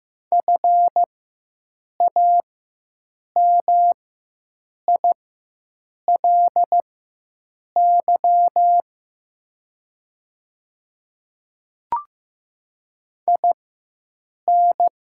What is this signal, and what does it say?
Morse code, heard as a keyed tone, about 15 wpm, spaced as for 7 wpm, 700 Hz. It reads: FAMILY IN
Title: 4x Character Spacing - Top 500 Words - 15wpm